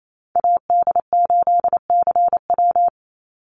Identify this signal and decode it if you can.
Morse code, heard as a keyed tone, about 28 wpm, 700 Hz. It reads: AD8CW
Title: Call Signs - 28wpm